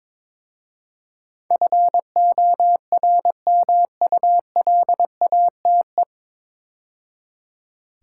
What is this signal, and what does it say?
Morse code, heard as a keyed tone, about 22 wpm, 700 Hz. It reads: FORMULATE
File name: 9 Letter Words - First 3 Letters Given - 22wpm